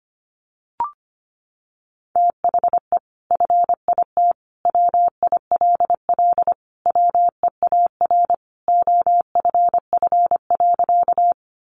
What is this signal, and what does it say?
Morse code, heard as a keyed tone, about 25 wpm, 700 Hz. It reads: THE FIT WILL WEAR OFF.
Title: Sentences from Top 1500 Words 25wpm